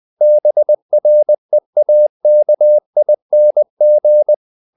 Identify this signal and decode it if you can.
Morse code, heard as a keyed tone, about 20 wpm, 600 Hz. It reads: BREAKING
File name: the old man and the sea part 2 20wpm